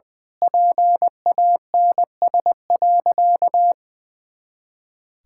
Morse code, heard as a keyed tone, about 20 wpm, 700 Hz. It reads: PANS.
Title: Pangram Challenge v2 - 20wpm